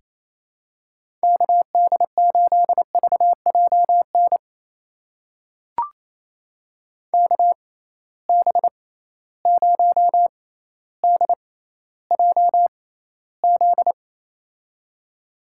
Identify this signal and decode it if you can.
Morse code, heard as a keyed tone, about 28 wpm, spaced as for 14 wpm, 700 Hz. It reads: KD8VJN KB0DJZ
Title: USA Call Signs - 6 Characters - 6x Character Spacing - 28wpm